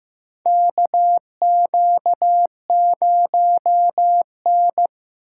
Morse code, heard as a keyed tone, about 15 wpm, 700 Hz. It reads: KQ0N
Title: USA Call Signs - 4 Characters - 8x Character Spacing - 15wpm